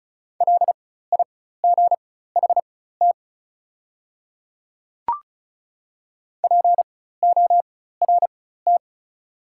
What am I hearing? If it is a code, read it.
Morse code, heard as a keyed tone, about 34 wpm, spaced as for 16 wpm, 700 Hz. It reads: LIGHT PORT
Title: 4x Character Spacing - Top 200 Words - 35wpm